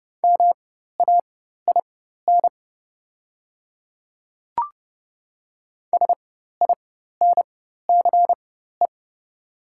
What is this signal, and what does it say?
Morse code, heard as a keyed tone, about 30 wpm, spaced as for 14 wpm, 700 Hz. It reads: MAIN SINCE
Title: 4x Character Spacing - Top 300 Words - 30wpm